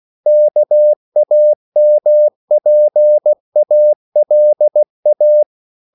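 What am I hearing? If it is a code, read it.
Morse code, heard as a keyed tone, about 16 wpm, 600 Hz. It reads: KAMPALA
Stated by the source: CW practice of national capitals at 16 words per minute, from ON6ZQ.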